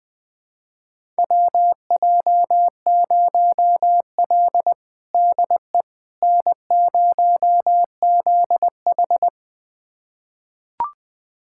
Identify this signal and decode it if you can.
Morse code, heard as a keyed tone, about 21 wpm, 700 Hz. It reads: WJ0L DE N0ZH
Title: USA Callsign DE Callsign - 4 Characters - 20wpm